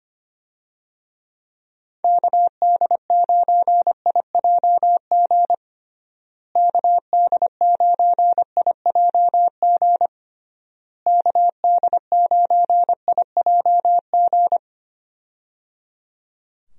Morse code, heard as a keyed tone, about 25 wpm, 700 Hz. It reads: KD9IJG KD9IJG KD9IJG
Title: USA Call Signs - 6 Characters - Repeated 3x - 25wpm